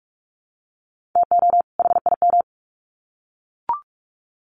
Morse code, heard as a keyed tone, about 44 wpm, 700 Hz. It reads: TO HIM
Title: Sets of 2 Words 45wpm